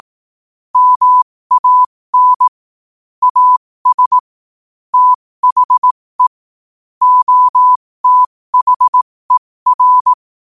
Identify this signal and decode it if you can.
Morse code, heard as a keyed tone, about 18 wpm, spaced as for 15 wpm, 1000 Hz. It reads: MAN AS THE OTHER
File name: qotdfiles / 15wpm